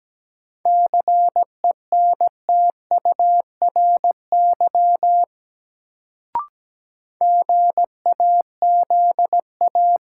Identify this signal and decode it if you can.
Morse code, heard as a keyed tone, about 17 wpm, 700 Hz. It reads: CENTURY GAZA
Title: Sets of 5 Words - New Word Drill - 17wpm